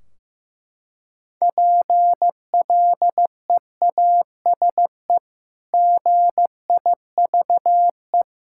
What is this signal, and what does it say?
Morse code, heard as a keyed tone, about 15 wpm, 700 Hz. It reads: PLEASE GIVE
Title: Sentences from Top 1600 Words 15wpm